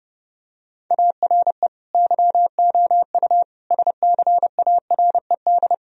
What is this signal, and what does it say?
Morse code, heard as a keyed tone, about 32 wpm, 700 Hz. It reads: ARE YOU SCARED
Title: Sentences from Top 1000 Words - Review Part 4 - 30wpm